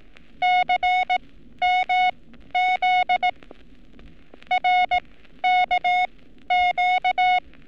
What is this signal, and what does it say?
Morse code, heard as a keyed tone, about 18 wpm, spaced as for 12 wpm, 730 Hz. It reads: CMZ RKQ
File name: training / NAVPERS 11224-11235 / NAVPERS 11227 RA RB